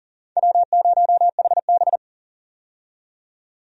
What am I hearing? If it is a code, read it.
Morse code, heard as a keyed tone, about 40 wpm, 700 Hz. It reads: W0HB
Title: USA Call Signs - 4 Characters - 40wpm